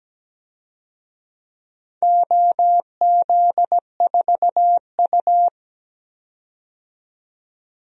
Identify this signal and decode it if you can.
Morse code, heard as a keyed tone, about 17 wpm, 700 Hz. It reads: OZ4U
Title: DXCC-100 Call Signs - 4 Characters - 17wpm